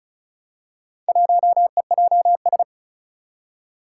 Morse code, heard as a keyed tone, about 34 wpm, 700 Hz. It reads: 1EJS
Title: Four Challenging Letter-Number Combinations 35wpm